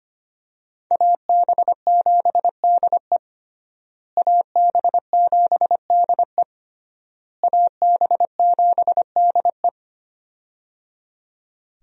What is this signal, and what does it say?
Morse code, heard as a keyed tone, about 25 wpm, 700 Hz. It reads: AB7DE AB7DE AB7DE
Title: USA Call Signs - 5 Characters - Repeated 3x - 25wpm